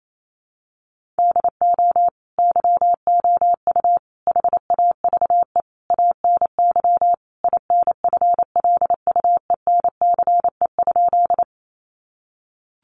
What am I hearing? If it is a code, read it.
Morse code, heard as a keyed tone, about 28 wpm, 700 Hz. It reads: DO YOU HAVE ANY INFLUENCE?
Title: Sentences from Top 1100 Words 28wpm